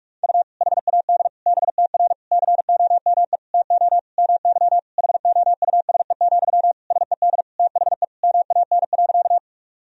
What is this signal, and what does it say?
Morse code, heard as a keyed tone, about 41 wpm, 700 Hz. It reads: U HAD BTR COME TO MY HOUSE, SED THE MAN.
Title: Sentences from Top 400 Words and Abbreviations 45wpm